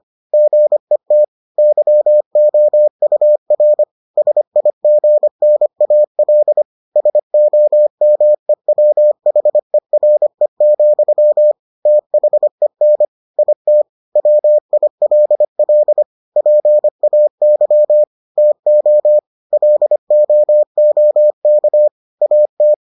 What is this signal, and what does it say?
Morse code, heard as a keyed tone, about 25 wpm, 600 Hz. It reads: GET YOUR SIGNAL SOMEWHERE, THEN IT WILL PAY TO LOOK AT